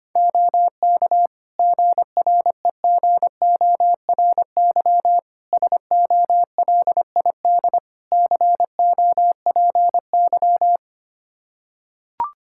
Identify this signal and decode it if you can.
Morse code, heard as a keyed tone, about 25 wpm, 700 Hz. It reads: OK GREGORY SOLID COPY
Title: Practice QSO - Exchange 3 - More Info - 25wpm